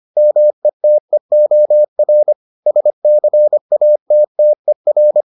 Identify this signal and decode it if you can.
Morse code, heard as a keyed tone, about 25 wpm, 600 Hz. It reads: METEOR SCATTER